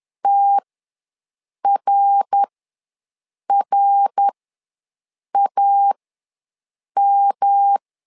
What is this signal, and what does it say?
Morse code, heard as a keyed tone, about 11 wpm, spaced as for 6 wpm, 800 Hz. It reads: TRRAM
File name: LEZIONI / 3 MWRO / 3 MWR ws 10wpm 2151